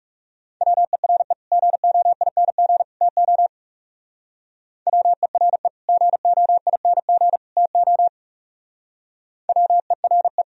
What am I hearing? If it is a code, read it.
Morse code, heard as a keyed tone, about 35 wpm, 690 Hz. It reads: WERE GOING TO WERE GOING TO WERE
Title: Speed-Racing - Sets of 3 Words 30wpm